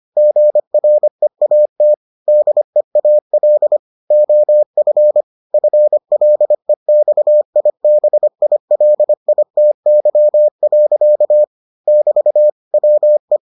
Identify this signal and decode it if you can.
Morse code, heard as a keyed tone, about 25 wpm, 600 Hz. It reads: GREAT DEAL OF FLEXIBILITY. <BT> WE